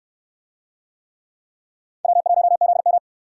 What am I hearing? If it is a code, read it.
Morse code, heard as a keyed tone, about 69 wpm, 700 Hz. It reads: W1ZM